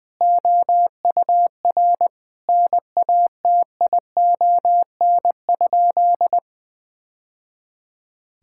Morse code, heard as a keyed tone, about 20 wpm, 700 Hz. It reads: OUR NATION?
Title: Sentences from Top 1000 Words - Review Part 6 - 20wpm